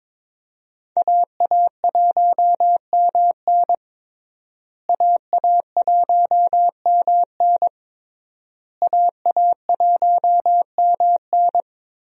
Morse code, heard as a keyed tone, about 22 wpm, 700 Hz. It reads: AA1MN AA1MN AA1MN
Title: USA Call Signs - 5 Characters - Repeated 3x - 22wpm